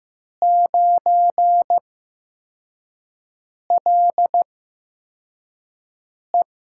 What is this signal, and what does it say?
Morse code, heard as a keyed tone, about 15 wpm, spaced as for 4 wpm, 700 Hz. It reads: 9LE
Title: USA Call Signs - 6 Characters - 8x Character Spacing - 15wpm